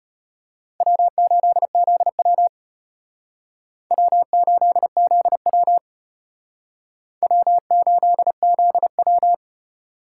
Morse code, heard as a keyed tone, about 32 wpm, 700 Hz. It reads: W8ZW W8ZW W8ZW